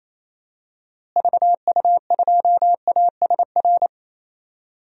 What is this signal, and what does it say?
Morse code, heard as a keyed tone, about 28 wpm, 700 Hz. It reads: VU2ASR
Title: DXCC-100 Call Signs - 6 Characters - Repeated 2x - 28wpm